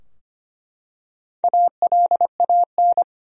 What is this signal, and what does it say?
Morse code, heard as a keyed tone, about 25 wpm, 700 Hz. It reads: ALAN
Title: USA Common Names - Max 4 Letters - 25wpm